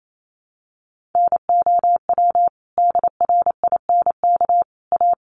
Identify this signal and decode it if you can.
Morse code, heard as a keyed tone, about 28 wpm, 700 Hz. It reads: NOW DRINK A
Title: Sentences from Top 1500 Words 28wpm